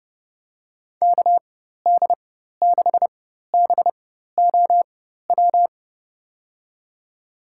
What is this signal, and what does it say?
Morse code, heard as a keyed tone, about 30 wpm, spaced as for 14 wpm, 700 Hz. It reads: KD6BOW
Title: USA Call Signs - 4x Character Spacing - 30wpm